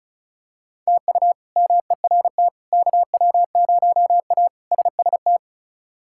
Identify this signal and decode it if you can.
Morse code, heard as a keyed tone, about 36 wpm, 700 Hz. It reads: TU MERT KW0A SST
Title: K1USN SST QSO Practice - 35wpm